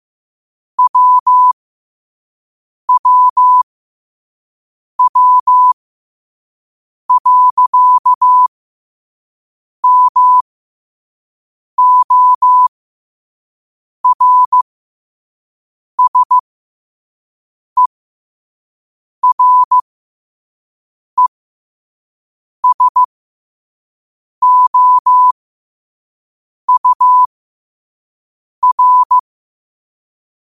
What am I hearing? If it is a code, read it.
Morse code, heard as a keyed tone, about 15 wpm, spaced as for 5 wpm, 1000 Hz. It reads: WWW.MORSERESOUR
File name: qotdfiles / 5wpm